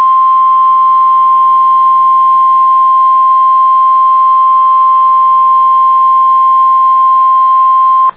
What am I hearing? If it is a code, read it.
Emergency Alert System (SAME) tones (the 1050 Hz attention tone). It is heard from 0.4 to 7.2 s.